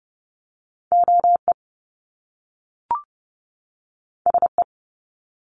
Morse code, heard as a keyed tone, about 30 wpm, 700 Hz. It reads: O E S E
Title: Top Two Letter Combinations in Words 30wpm